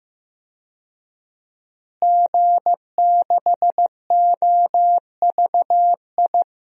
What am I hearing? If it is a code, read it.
Morse code, heard as a keyed tone, about 15 wpm, 700 Hz. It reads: G6OVI
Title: DXCC-100 Call Signs - 5 Characters - Repeated 2x - 15wpm